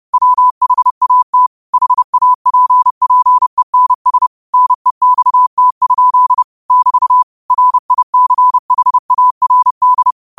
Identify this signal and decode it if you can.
Morse code, heard as a keyed tone, about 30 wpm, 1000 Hz. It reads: WHAT HAPPENS NEXT? <BT> RICHARD
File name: qotdfiles / 30wpm